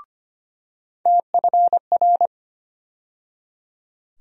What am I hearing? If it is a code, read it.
Morse code, heard as a keyed tone, about 25 wpm, 700 Hz. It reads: TFR
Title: USA Call Sign Postfixes - 3 Letters 25wpm